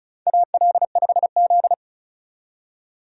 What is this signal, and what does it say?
Morse code, heard as a keyed tone, about 34 wpm, 700 Hz. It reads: AL5Z
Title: USA Call Signs - 4 Characters - 35wpm